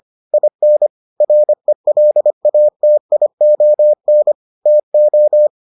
Morse code, heard as a keyed tone, about 25 wpm, 600 Hz. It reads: IN RELATION TO